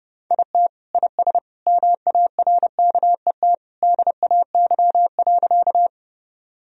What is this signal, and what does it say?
Morse code, heard as a keyed tone, about 29 wpm, 700 Hz. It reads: IT IS MARKET DAY.
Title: Sentences from Top 1000 Words - Review Part 3 - 30wpm